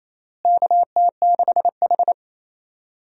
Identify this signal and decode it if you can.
Morse code, heard as a keyed tone, about 28 wpm, 700 Hz. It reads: KT6H
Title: USA Call Signs - 4 Characters - Repeated 3x - 28wpm